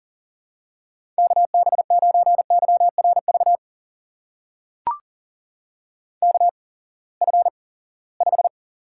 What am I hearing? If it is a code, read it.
Morse code, heard as a keyed tone, about 40 wpm, spaced as for 23 wpm, 700 Hz. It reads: KB9YRV KF5
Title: USA Call Signs - 6 Characters - 8x Character Spacing - 40wpm